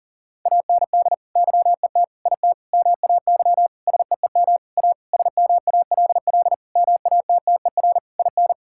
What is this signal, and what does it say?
Morse code, heard as a keyed tone, about 40 wpm, 700 Hz. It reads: AND YET IT MAY SEEM A SMALL MATTER IN
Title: Sentences from Top 1000 Words - Review Part 5 - 40wpm